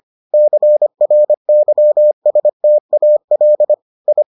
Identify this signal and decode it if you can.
Morse code, heard as a keyed tone, about 25 wpm, 600 Hz. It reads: CRYSTAL I